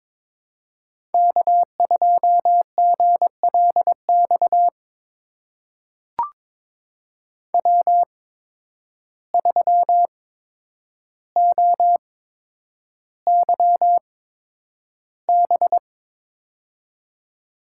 Morse code, heard as a keyed tone, about 22 wpm, spaced as for 9 wpm, 700 Hz. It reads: K2GLX W3OYB
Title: USA Call Signs - 5 Characters - 8x Character Spacing - 22wpm